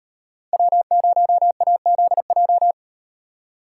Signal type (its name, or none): Morse code